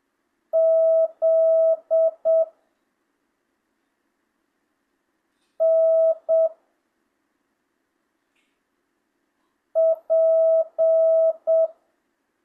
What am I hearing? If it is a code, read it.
Morse code, heard as a keyed tone, about 7 wpm, 650 Hz. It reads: Z N P
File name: jh8cbh / 2025cw